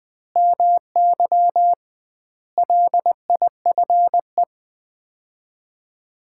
Morse code, heard as a keyed tone, about 20 wpm, 700 Hz. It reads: MY LIFE
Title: 2x Word-Spacing - Sets of 5 Words - 20wpm